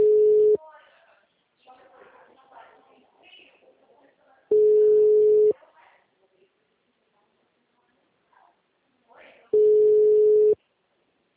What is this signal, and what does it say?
Ringback tone (Europe, 425 Hz), with speech.
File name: out- -2010- - - .13637